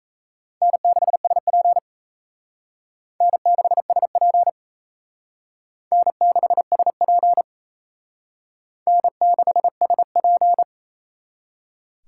Morse code, heard as a keyed tone, about 34 wpm, 700 Hz. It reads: N6SP N6SP N6SP N6SP
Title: Speed Racing - USA Call Signs 28wpm